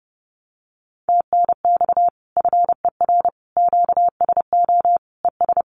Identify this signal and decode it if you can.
Morse code, heard as a keyed tone, about 30 wpm, 700 Hz. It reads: TNX FER QSO ES
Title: Common QSO Phrases Encore 30wpm